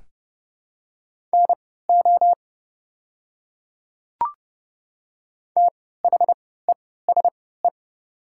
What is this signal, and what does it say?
Morse code, heard as a keyed tone, about 30 wpm, spaced as for 17 wpm, 690 Hz. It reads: NO THESE
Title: 3x Character Spacing - Top 500 Words - 30wpm